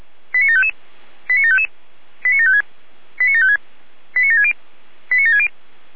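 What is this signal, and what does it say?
Five-tone selective call (ZVEI), heard as quick runs of five tones, tones of about 70 ms. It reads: 89755 89755 89756 89756 89760 89760
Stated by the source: Received on an SDR.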